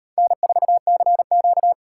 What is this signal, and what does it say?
Morse code, heard as a keyed tone, about 38 wpm, 700 Hz. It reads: N4CQ